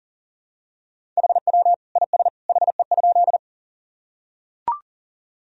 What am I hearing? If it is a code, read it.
Morse code, heard as a keyed tone, about 40 wpm, 700 Hz. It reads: HW IS HE?
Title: Sentences from Top 100 Words and Abbreviations 40wpm